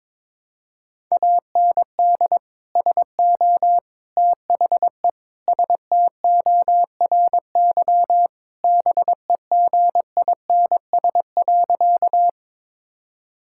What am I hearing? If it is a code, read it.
Morse code, heard as a keyed tone, about 22 wpm, 700 Hz. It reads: AND SO THE STORY BEGINS.
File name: practice / 22wpm / Sentences with 2-Word Phrases - 22wpm